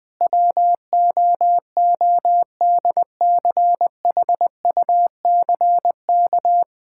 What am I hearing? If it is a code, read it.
Morse code, heard as a keyed tone, about 21 wpm, 700 Hz. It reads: WOODCHUCK
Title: Pangram Challenge v2 - 20wpm